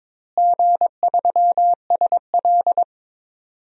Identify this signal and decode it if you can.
Morse code, heard as a keyed tone, about 22 wpm, 700 Hz. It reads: G3SL
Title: DXCC-100 Call Signs - 4 Characters - Repeated 2x - 22wpm